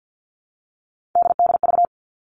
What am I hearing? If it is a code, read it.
Morse code, heard as a keyed tone, about 50 wpm, 700 Hz. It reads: DDV
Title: All Cut Numbers - 3-Digit 50wpm